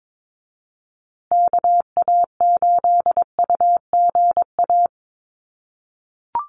Morse code, heard as a keyed tone, about 22 wpm, 700 Hz. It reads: KA8UGA
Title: USA Call Signs Repeated 2x 22wpm